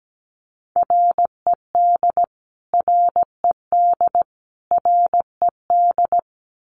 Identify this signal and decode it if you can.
Morse code, heard as a keyed tone, about 17 wpm, 700 Hz. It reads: RED RED RED
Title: Top 500 Words Repeated 17wpm